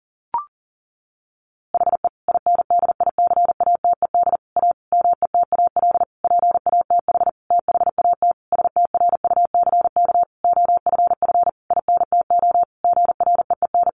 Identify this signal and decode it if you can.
Morse code, heard as a keyed tone, about 40 wpm, 700 Hz. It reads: HE INDICATED A METAL PATH THAT STRUCK OFF INTO GREEN
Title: A Sound Of Thunder 40wpm